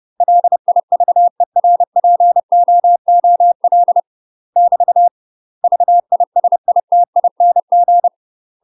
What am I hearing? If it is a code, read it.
Morse code, heard as a keyed tone, about 30 wpm, 700 Hz. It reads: LIVERPOOL <BT> VISITING
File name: UK MASTER QSOs TRAINING / 30WPM / UK MASTER QSOs 0003